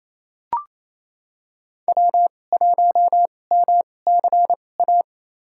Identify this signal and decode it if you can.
Morse code, heard as a keyed tone, about 28 wpm, spaced as for 20 wpm, 700 Hz. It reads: W1MCA
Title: USA Call Signs - 5 Characters - 2x Character Spacing - 28wpm